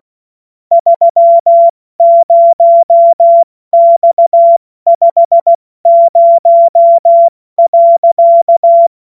Morse code, heard as a keyed tone, about 16 wpm, spaced as for 14 wpm, 680 Hz. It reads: 30X50.
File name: Morsekurs / Witze / Witz 20. Januar 2026 70 F80